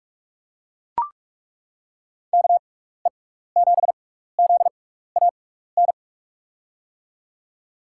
Morse code, heard as a keyed tone, about 41 wpm, spaced as for 15 wpm, 700 Hz. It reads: KE7ZAN
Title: USA Call Signs - 6x Character Spacing - 45wpm